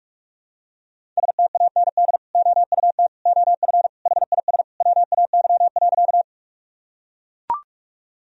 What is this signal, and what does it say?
Morse code, heard as a keyed tone, about 42 wpm, 700 Hz. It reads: STAND OUT OF HIS WAY.
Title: Sentences from Top 1000 Words - Review Part 1 - 45wpm